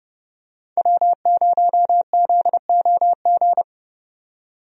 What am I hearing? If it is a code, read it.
Morse code, heard as a keyed tone, about 30 wpm, 700 Hz. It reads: W0ZOG
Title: USA Call Signs - 5 Characters - 30wpm